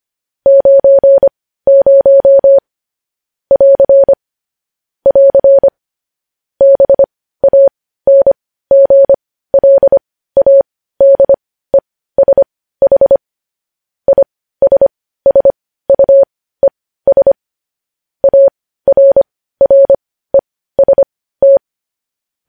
Morse code, heard as a keyed tone, about 25 wpm, spaced as for 15 wpm, 550 Hz. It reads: 90 <AR> <AR> BANGLADESH ISSUES ARREST
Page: news headlines in morse code at 15 wpm